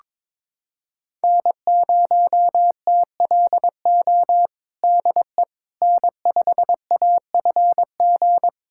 Morse code, heard as a keyed tone, about 22 wpm, 700 Hz. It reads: N0TLO DE N5AFG
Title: USA Callsign DE Callsign - 5 Characters - 22wpm